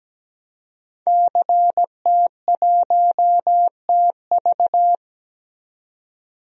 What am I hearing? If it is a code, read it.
Morse code, heard as a keyed tone, about 17 wpm, 700 Hz. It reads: CT1TV
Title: DXCC-100 Call Signs - 5 Characters - 17wpm